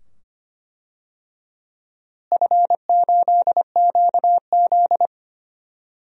Morse code, heard as a keyed tone, about 25 wpm, 700 Hz. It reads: F8QZ
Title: DXCC-100 Call Signs - 4 Characters - Repeated 2x - 25wpm